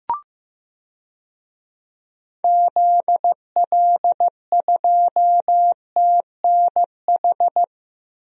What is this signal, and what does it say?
Morse code, heard as a keyed tone, about 15 wpm, 700 Hz. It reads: ZL2TNH
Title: DXCC-100 Call Signs - 6 Characters - Repeated 2x - 15wpm